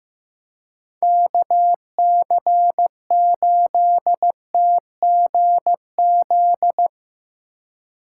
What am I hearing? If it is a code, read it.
Morse code, heard as a keyed tone, about 17 wpm, 700 Hz. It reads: KC8TGZ
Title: Speed Racing - USA 2x3 Call Signs 15wpm